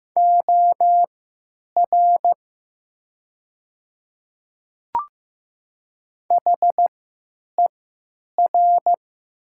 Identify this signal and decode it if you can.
Morse code, heard as a keyed tone, about 15 wpm, spaced as for 9 wpm, 700 Hz. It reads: OR HER
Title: 3x Character Spacing - Top 100 Words - 15wpm